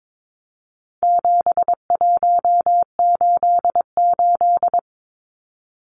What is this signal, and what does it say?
Morse code, heard as a keyed tone, about 22 wpm, 700 Hz. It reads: 7188
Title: Numbers - 4-Digit 22wpm